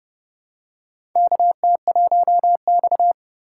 Morse code, heard as a keyed tone, about 32 wpm, 700 Hz. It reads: KT1X
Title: USA Call Signs - 4 Characters - 2x Character Spacing - 30wpm